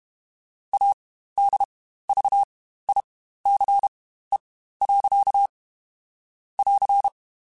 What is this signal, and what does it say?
Morse code, heard as a keyed tone, about 32 wpm, spaced as for 15 wpm, 800 Hz. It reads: ADVICE. <AR>
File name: Quotes Spaced / Quotes Spaced-mp3s / Advice sp @30